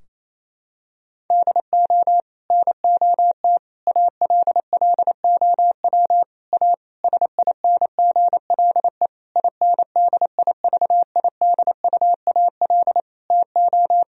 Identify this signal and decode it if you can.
Morse code, heard as a keyed tone, about 28 wpm, 700 Hz. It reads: DO NOT ALLOW A SINGLE INDIVIDUAL TO
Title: Sentences from Top 1000 Words - Review Part 4 - 28wpm